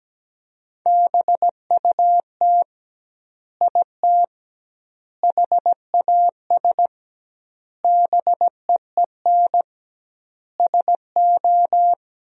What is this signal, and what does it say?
Morse code, heard as a keyed tone, about 17 wpm, 700 Hz. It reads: BUT IT HAS BEEN SO
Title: 2x Word-Spacing - Sentences from Top 100 Words - 17wpm